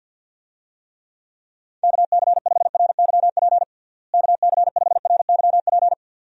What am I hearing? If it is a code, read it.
Morse code, heard as a keyed tone, about 51 wpm, 700 Hz. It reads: KC5RYP KC5RYP
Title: USA Call Signs - 6 Characters - Repeated 2x - 50wpm